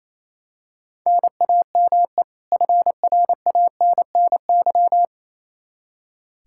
Morse code, heard as a keyed tone, about 28 wpm, 700 Hz. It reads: NAME FRANNY
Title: Practice QSO - Exchange 1 - Beginning - 28wpm